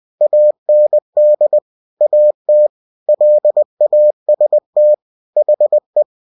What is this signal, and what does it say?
Morse code, heard as a keyed tone, about 20 wpm, 600 Hz. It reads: AND AT LAST HE
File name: cw-books / winnie-the-pooh / winnie-20wpm-page-0015